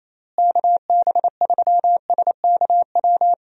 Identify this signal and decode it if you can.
Morse code, heard as a keyed tone, about 28 wpm, 700 Hz. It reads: KB3SKW
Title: USA Call Signs - 6 Characters - Repeated 3x - 28wpm